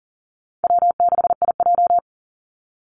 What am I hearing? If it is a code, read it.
Morse code, heard as a keyed tone, about 40 wpm, 700 Hz. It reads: W6IJ
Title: Call Signs Repeated 2x - 40wpm